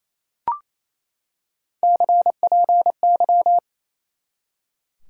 Morse code, heard as a keyed tone, about 28 wpm, 700 Hz. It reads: CPY
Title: Top 90 QSO Elements 28wpm